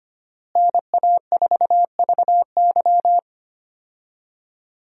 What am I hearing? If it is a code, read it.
Morse code, heard as a keyed tone, about 25 wpm, 700 Hz. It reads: NA4VY
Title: USA Call Signs - 5 Characters - Repeated 2x - 25wpm